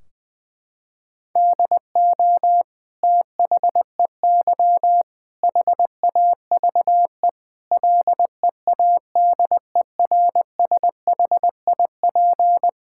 Morse code, heard as a keyed tone, about 20 wpm, 700 Hz. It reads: DO THEY HAVE LEADERSHIP
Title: Sentences from Top 1600 Words 20wpm